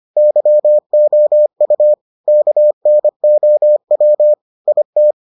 Morse code, heard as a keyed tone, about 25 wpm, 600 Hz. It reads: YOU KNOW IT